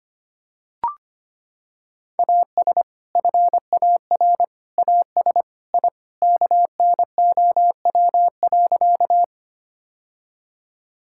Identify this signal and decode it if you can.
Morse code, heard as a keyed tone, about 25 wpm, 700 Hz. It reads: AS FAR AS I KNOW.
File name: practice / 25wpm / Sentences with 3-Word Phrases - 25wpm